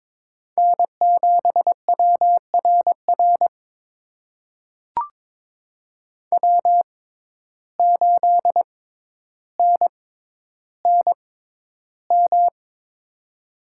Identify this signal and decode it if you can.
Morse code, heard as a keyed tone, about 22 wpm, spaced as for 11 wpm, 700 Hz. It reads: N7WRR W8NNM
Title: USA Call Signs - 5 Characters - 6x Character Spacing - 22wpm